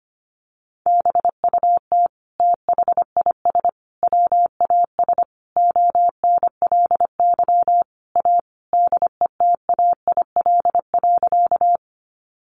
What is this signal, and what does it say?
Morse code, heard as a keyed tone, about 25 wpm, 700 Hz. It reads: BUT THIS WAS ONLY A DETAIL.
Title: Sentences from Top 1500 Words 25wpm